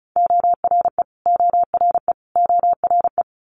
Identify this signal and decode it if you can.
Morse code, heard as a keyed tone, about 36 wpm, 700 Hz. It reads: ORE ORE ORE
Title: Multi-letter Phonemes Repeated 3x 35wpm